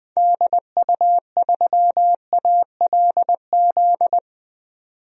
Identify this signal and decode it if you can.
Morse code, heard as a keyed tone, about 21 wpm, 700 Hz. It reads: DU3ALZ
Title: DXCC-100 Call Signs - 6 Characters - 20wpm